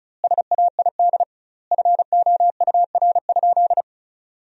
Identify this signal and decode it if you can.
Morse code, heard as a keyed tone, about 36 wpm, 700 Hz. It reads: SAID FOUR?
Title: 2x Word-Spacing - Sentences from Top 300 Words - 35wpm